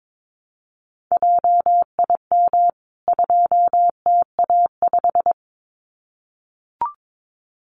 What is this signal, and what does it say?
Morse code, heard as a keyed tone, about 22 wpm, 700 Hz. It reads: JIM 2TA5
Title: CWT Exchange 22wpm